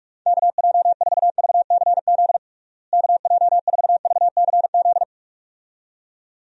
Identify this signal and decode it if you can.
Morse code, heard as a keyed tone, about 46 wpm, 700 Hz. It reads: KJ4VCZ KJ4VCZ
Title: USA Call Signs - 6 Characters - Repeated 2x - 45wpm